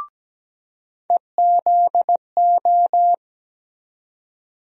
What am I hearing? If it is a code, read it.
Morse code, heard as a keyed tone, about 17 wpm, 700 Hz. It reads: EZO
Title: USA Call Sign Postfixes - 3 Letters 17wpm